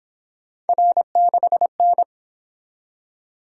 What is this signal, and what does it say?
Morse code, heard as a keyed tone, about 28 wpm, 700 Hz. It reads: R6N
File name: Speed-Racing - Three Letter-Number Combinations - 22wpm